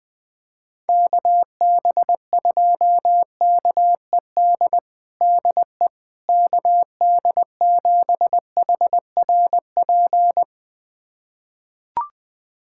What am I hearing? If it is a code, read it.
Morse code, heard as a keyed tone, about 20 wpm, 700 Hz. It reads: KB2KED DE KD7HRP
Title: USA Callsign DE Callsign 20wpm